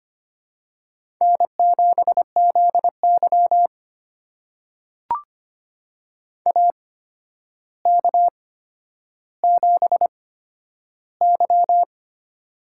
Morse code, heard as a keyed tone, about 25 wpm, spaced as for 11 wpm, 700 Hz. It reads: N7ZY AK7Y
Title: USA Call Signs - 4 Characters - 8x Character Spacing - 25wpm